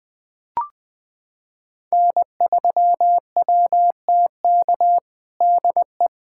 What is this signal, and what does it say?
Morse code, heard as a keyed tone, about 20 wpm, 700 Hz. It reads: N3WTK DE
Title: USA Callsign DE Callsign 20wpm